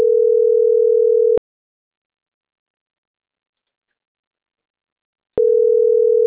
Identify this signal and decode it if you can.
Ringback tone (North America), 440 + 480 Hz.